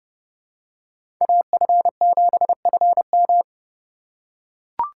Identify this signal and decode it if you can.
Morse code, heard as a keyed tone, about 30 wpm, 700 Hz. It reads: AF7FM
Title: USA Call Signs - 5 Characters - 4x Character Spacing - 30wpm